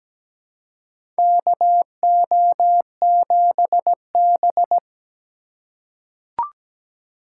Morse code, heard as a keyed tone, about 17 wpm, 700 Hz. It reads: KO7B
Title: USA Call Signs - 3x Character Spacing - 17wpm